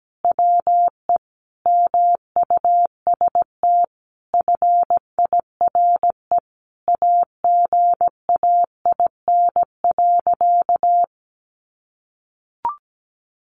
Morse code, heard as a keyed tone, about 17 wpm, 700 Hz. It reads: WE MUST FIRE AGAIN.
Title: Sentences from Top 1400 Words 17wpm